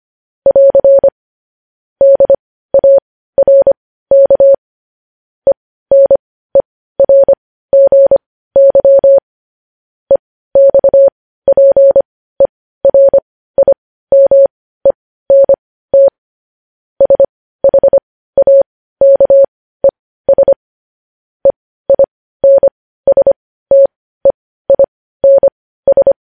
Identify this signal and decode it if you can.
Morse code, heard as a keyed tone, about 25 wpm, spaced as for 15 wpm, 550 Hz. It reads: <AR> DARK ENERGY EXPERIMENT SHAKES EINSTEINS
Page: news headlines in morse code at 15 wpm